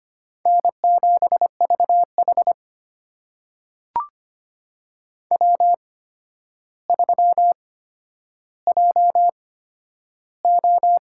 Morse code, heard as a keyed tone, about 25 wpm, spaced as for 11 wpm, 700 Hz. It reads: N7VH W3JO
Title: USA Call Signs - 4 Characters - 8x Character Spacing - 25wpm